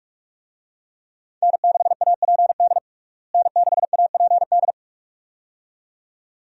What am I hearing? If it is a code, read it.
Morse code, heard as a keyed tone, about 46 wpm, 700 Hz. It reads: N6APD N6APD
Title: USA Call Signs - 5 Characters - Repeated 2x - 45wpm